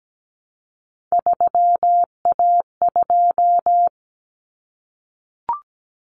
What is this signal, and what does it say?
Morse code, heard as a keyed tone, about 17 wpm, 700 Hz. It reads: 3A2
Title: Common Cut Numbers - 3-Digit 17wpm